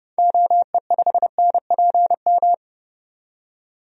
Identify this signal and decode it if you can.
Morse code, heard as a keyed tone, about 32 wpm, 700 Hz. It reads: OE5NPM
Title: DXCC-100 Call Signs - 6 Characters - 30wpm